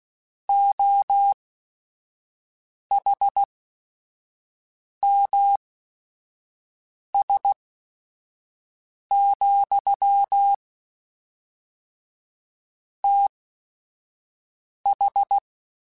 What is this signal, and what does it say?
Morse code, heard as a keyed tone, about 16 wpm, spaced as for 5 wpm, 800 Hz. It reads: OHMS, TH
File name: Technical Subjects / Technical Subjects-mp3s / XC @5